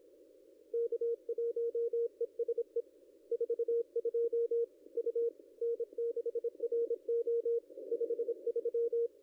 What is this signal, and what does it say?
Morse code, heard as a keyed tone, about 25 wpm, 470 Hz. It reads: K1ESE 42 U N6RO 53
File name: contests / ss-cw-k5tr-op-n5rz -0800